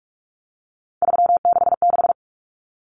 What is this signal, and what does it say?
Morse code, heard as a keyed tone, about 44 wpm, 700 Hz. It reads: 366
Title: Common Cut Numbers - 3-Digit 45wpm